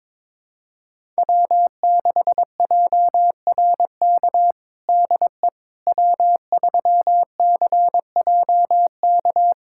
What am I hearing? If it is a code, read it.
Morse code, heard as a keyed tone, about 22 wpm, 700 Hz. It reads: W6JRK DE W3CJK
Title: USA Callsign DE Callsign - 5 Characters - 22wpm